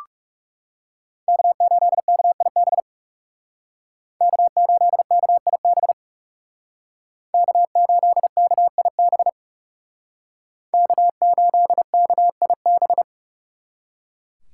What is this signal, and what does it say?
Morse code, heard as a keyed tone, about 39 wpm, 700 Hz. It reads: K8KIB K8KIB K8KIB K8KIB